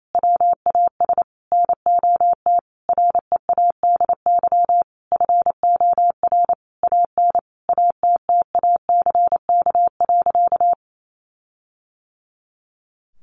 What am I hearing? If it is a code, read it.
Morse code, heard as a keyed tone, about 28 wpm, 700 Hz. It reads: WAS NOT READY FOR AN ATTACK.
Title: Sentences from Top 1500 Words 28wpm